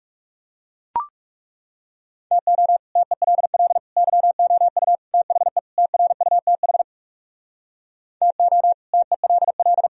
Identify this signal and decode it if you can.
Morse code, heard as a keyed tone, about 40 wpm, 700 Hz. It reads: TO TELL YOU THE TRUTH TO TELL